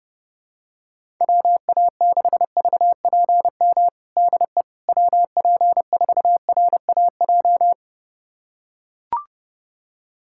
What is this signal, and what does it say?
Morse code, heard as a keyed tone, about 30 wpm, 700 Hz. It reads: WA6VPM DE WP4RAJ
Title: USA Callsign DE Callsign - 6 Characters - 30wpm